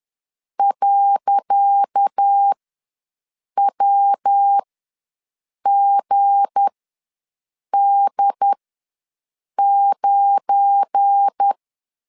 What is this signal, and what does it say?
Morse code, heard as a keyed tone, about 10 wpm, spaced as for 6 wpm, 800 Hz. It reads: .WGD9
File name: LEZIONI / 10 SIGNS / 10D COMA ws 10wpm 4314